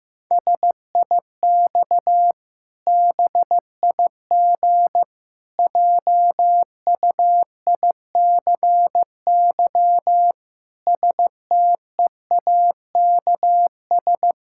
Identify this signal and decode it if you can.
Morse code, heard as a keyed tone, about 15 wpm, 700 Hz. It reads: SIX BIG JUICY STEAKS
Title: Pangram Challenge v2 - 15wpm